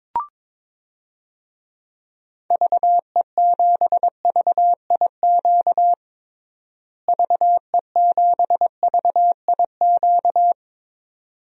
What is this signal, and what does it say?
Morse code, heard as a keyed tone, about 22 wpm, 700 Hz. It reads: VE7VIQ VE7VIQ
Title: DXCC-100 Call Signs - 6 Characters - Repeated 2x - 22wpm